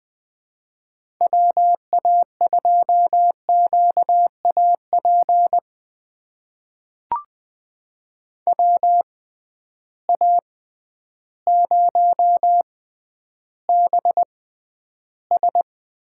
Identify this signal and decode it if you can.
Morse code, heard as a keyed tone, about 20 wpm, spaced as for 11 wpm, 700 Hz. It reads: WA2QAP WA0BS
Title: USA Call Signs - 6 Characters - 6x Character Spacing - 20wpm